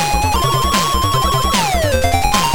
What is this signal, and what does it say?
There is no code.